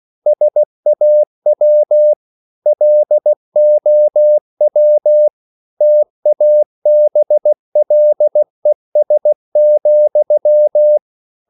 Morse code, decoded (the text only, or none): SAW LOW TABLES,